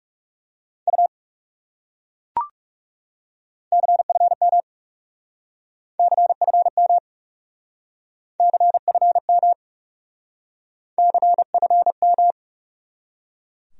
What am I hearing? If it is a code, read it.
Morse code, heard as a keyed tone, about 39 wpm, 700 Hz. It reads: U CFM CFM CFM CFM